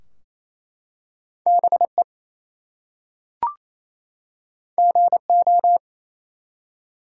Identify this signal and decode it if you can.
Morse code, heard as a keyed tone, about 27 wpm, 700 Hz. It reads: BE GO
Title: Two Letter Words - 28wpm